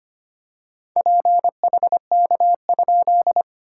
Morse code, heard as a keyed tone, about 25 wpm, 660 Hz. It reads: PHK?
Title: SOTA QSO Practice 25wpm